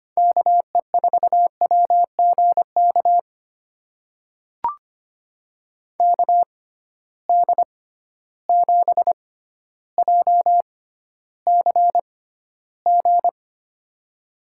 Morse code, heard as a keyed tone, about 25 wpm, spaced as for 13 wpm, 700 Hz. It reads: KE4WGK KD7JCG
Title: USA Call Signs - 6 Characters - 6x Character Spacing - 25wpm